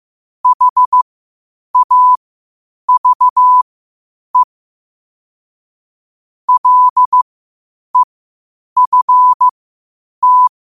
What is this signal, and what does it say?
Morse code, heard as a keyed tone, about 15 wpm, spaced as for 8 wpm, 1000 Hz. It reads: HAVE LEFT
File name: qotdfiles / 8wpm